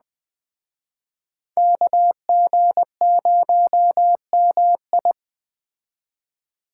Morse code, heard as a keyed tone, about 21 wpm, 700 Hz. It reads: KG0MI